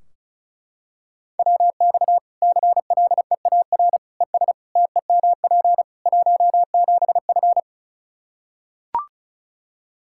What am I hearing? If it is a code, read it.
Morse code, heard as a keyed tone, about 34 wpm, 700 Hz. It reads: WX CLEAR ES TEMP 17F
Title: Practice QSO - Exchange 2 - Info - 35wpm